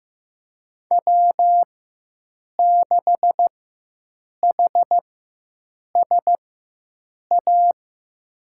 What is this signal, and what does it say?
Morse code, heard as a keyed tone, about 16 wpm, spaced as for 7 wpm, 700 Hz. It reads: W6HSA
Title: USA Call Signs - 5 Characters - 4x Character Spacing - 15wpm